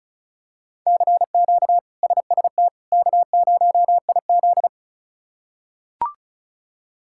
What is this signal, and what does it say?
Morse code, heard as a keyed tone, about 36 wpm, 700 Hz. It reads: CQ SST K0IZ
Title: K1USN SST QSO Practice - 35wpm